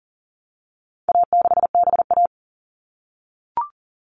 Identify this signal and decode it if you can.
Morse code, heard as a keyed tone, about 40 wpm, 700 Hz. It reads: A6BA
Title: All Cut Numbers - 4-Digit 40wpm